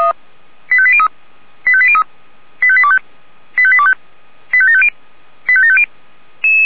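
Five-tone selective call (ZVEI), heard as quick runs of five tones, tones of about 70 ms. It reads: 85803 85803 85825 85825 85750 85750
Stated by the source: ZVEI five-tone selective call received by SDR.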